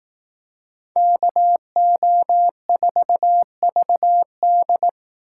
Morse code, heard as a keyed tone, about 19 wpm, 700 Hz. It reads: KO4VD